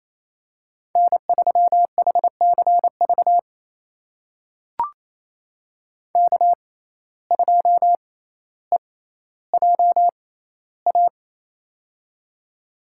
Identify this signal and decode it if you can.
Morse code, heard as a keyed tone, about 28 wpm, spaced as for 14 wpm, 700 Hz. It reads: N3HCV K2EJA
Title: USA Call Signs - 5 Characters - 6x Character Spacing - 28wpm